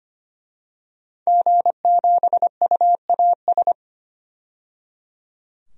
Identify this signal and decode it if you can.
Morse code, heard as a keyed tone, about 25 wpm, 700 Hz. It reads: G7UAS
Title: DXCC-100 Call Signs - 5 Characters - 25wpm